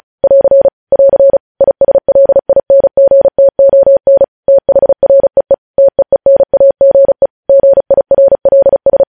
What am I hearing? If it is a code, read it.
Morse code, heard as a keyed tone, about 36 wpm, 550 Hz. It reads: <AR> <AR> ISLINGTON THREE TEENAGE GIRLS